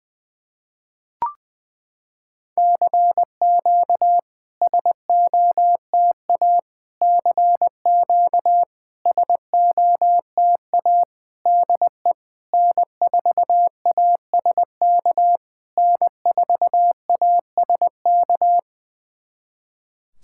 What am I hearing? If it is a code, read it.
Morse code, heard as a keyed tone, about 20 wpm, 700 Hz. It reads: CQ SOTA CQ SOTA DE N4ASK N4ASK
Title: SOTA QSO Practice 20wpm